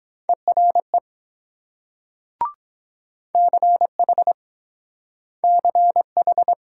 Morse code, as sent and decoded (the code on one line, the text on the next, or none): . .-. / -.-. .... / -.-. ....
ERE CH CH